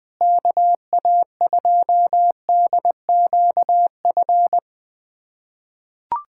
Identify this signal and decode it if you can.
Morse code, heard as a keyed tone, about 20 wpm, 700 Hz. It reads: KA2DQF
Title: USA Callsign DE Callsign - 6 Characters - 20wpm